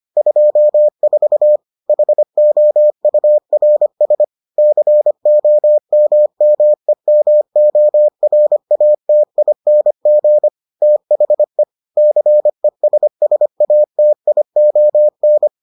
Morse code, heard as a keyed tone, about 25 wpm, 600 Hz. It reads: 24 HOURS COMMEMORATING THE CESSATION